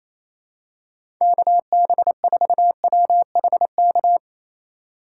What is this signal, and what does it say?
Morse code, heard as a keyed tone, about 28 wpm, 700 Hz. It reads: KB4WHK
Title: USA Call Signs - 6 Characters - 2x Character Spacing - 28wpm